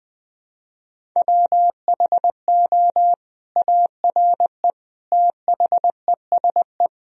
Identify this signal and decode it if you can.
Morse code, heard as a keyed tone, about 21 wpm, 700 Hz. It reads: WHO ARE THESE
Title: Sentences from Top 1000 Words - Review Part 1 - 20wpm